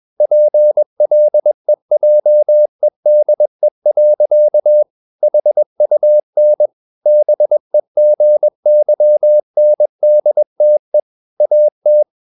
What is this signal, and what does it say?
Morse code, heard as a keyed tone, about 21 wpm, 600 Hz. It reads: PLEJEDE. HUN BEGYNDTE AT
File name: Ferie QRS 21wpm